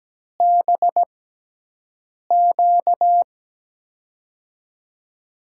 Morse code, heard as a keyed tone, about 17 wpm, spaced as for 6 wpm, 700 Hz. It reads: BQ
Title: USA Call Signs - 6 Characters - 6x Character Spacing - 17wpm